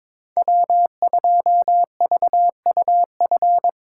Morse code, heard as a keyed tone, about 22 wpm, 700 Hz. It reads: W2VUF
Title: K1USN SST QSO Practice - 22wpm